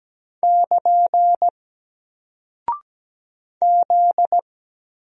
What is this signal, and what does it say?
Morse code, heard as a keyed tone, about 17 wpm, 700 Hz. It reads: <KN> Z
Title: Single Character (Letter:Number:Punctuation:Prosign) Alt 17wpm